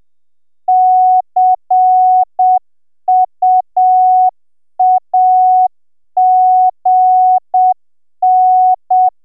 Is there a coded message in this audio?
Yes: Morse code from 0.7 to 9.1 s, heard as a keyed tone, about 6 wpm, 740 Hz. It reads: CUAGN